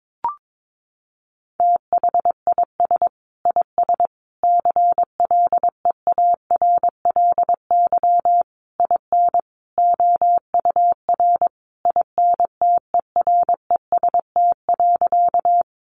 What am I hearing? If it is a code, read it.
Morse code, heard as a keyed tone, about 22 wpm, 700 Hz. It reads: THIS IS CLEARLY IN OUR INTEREST.
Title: Sentences from Top 1300 Words 22wpm